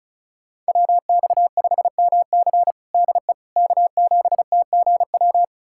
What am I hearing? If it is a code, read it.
Morse code, heard as a keyed tone, about 36 wpm, 700 Hz. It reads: WX5MC DE K7TGW
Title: USA Callsign DE Callsign - 5 Characters - 35wpm